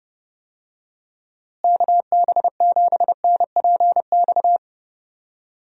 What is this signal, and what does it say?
Morse code, heard as a keyed tone, about 32 wpm, 700 Hz. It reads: KB7NPX
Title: USA Call Signs - 6 Characters - Repeated 3x - 30wpm